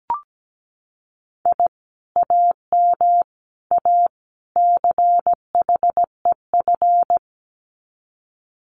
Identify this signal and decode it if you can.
Morse code, heard as a keyed tone, about 17 wpm, 700 Hz. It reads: I AM A CHEF
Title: Common QSO Phrases Encore 17wpm